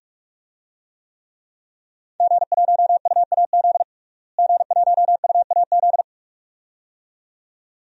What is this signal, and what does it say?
Morse code, heard as a keyed tone, about 46 wpm, 700 Hz. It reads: G1UAZ G1UAZ
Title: DXCC-100 Call Signs - 5 Characters - Repeated 2x - 45wpm